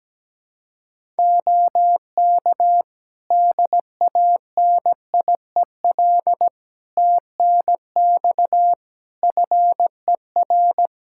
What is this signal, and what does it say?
Morse code, heard as a keyed tone, about 17 wpm, 700 Hz. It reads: OK DANIEL TNX FER
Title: Practice QSO - Exchange 4 - Ending - 17wpm